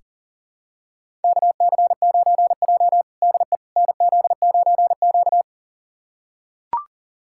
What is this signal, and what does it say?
Morse code, heard as a keyed tone, about 37 wpm, 700 Hz. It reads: KC9J DE NZ9Q
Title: USA Callsign DE Callsign - 4 Characters - 40wpm